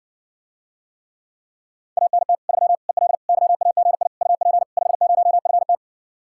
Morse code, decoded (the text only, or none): ANT 3 EL YAGI UP 50FT